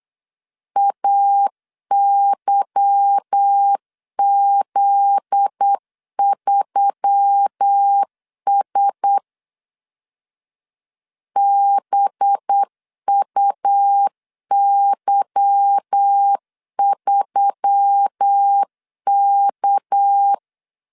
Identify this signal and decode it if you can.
Morse code, heard as a keyed tone, about 8 wpm, 800 Hz. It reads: AYZ3S BUY3K
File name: LEZIONI / 8 YJ45 / 8 Y 8wpm 3245